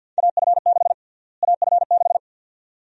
Morse code, heard as a keyed tone, about 51 wpm, 700 Hz. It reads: AF6 AF6
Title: USA Call Sign Prefixes - Repeated 2x - 50wpm